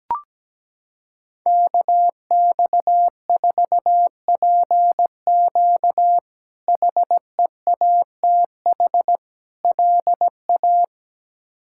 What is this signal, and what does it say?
Morse code, heard as a keyed tone, about 17 wpm, 700 Hz. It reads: KX4PQ HEATH LA
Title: K1USN SST QSO Practice - 17wpm